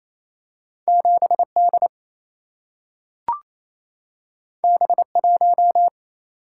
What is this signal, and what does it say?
Morse code, heard as a keyed tone, about 28 wpm, 700 Hz. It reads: 7D B1
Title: Single-Letter Single-Number 28wpm